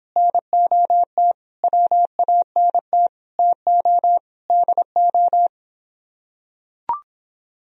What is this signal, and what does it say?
Morse code, heard as a keyed tone, about 26 wpm, 700 Hz. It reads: NOT WANT TO DO